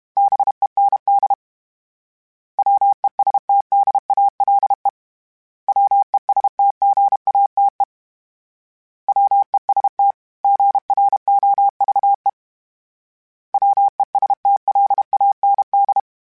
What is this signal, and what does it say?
Morse code, heard as a keyed tone, about 32 wpm, 800 Hz. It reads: BEND WESTDALE WESTGATE WEST GROVE WESTLAND
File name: fromSMs / FromSMs mp3s / Cities@30